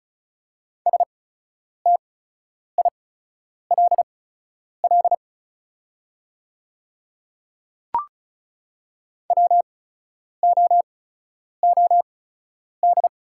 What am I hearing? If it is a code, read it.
Morse code, heard as a keyed tone, about 36 wpm, spaced as for 10 wpm, 700 Hz. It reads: STILL WOOD
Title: 8x Character Spacing - Top 300 Words - 35wpm